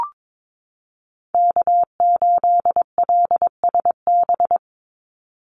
Morse code, heard as a keyed tone, about 22 wpm, 700 Hz. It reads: K8LSB
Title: Call Signs Repeated 3x - 22wpm